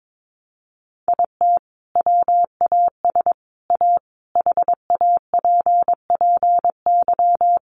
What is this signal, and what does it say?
Morse code, heard as a keyed tone, about 22 wpm, 700 Hz. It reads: IT WAS A HAPPY
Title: Sentences from Top 1100 Words 22wpm